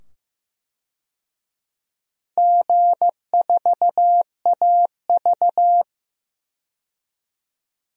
Morse code, heard as a keyed tone, about 15 wpm, 700 Hz. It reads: G4AV
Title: DXCC-100 Call Signs - 4 Characters - 15wpm